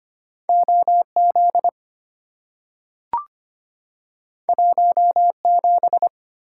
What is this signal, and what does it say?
Morse code, heard as a keyed tone, about 25 wpm, 700 Hz. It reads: OZ 17
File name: Two Letter-Number Combinations - 25wpm